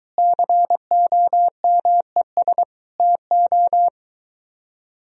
Morse code, heard as a keyed tone, about 23 wpm, 700 Hz. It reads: COMES TO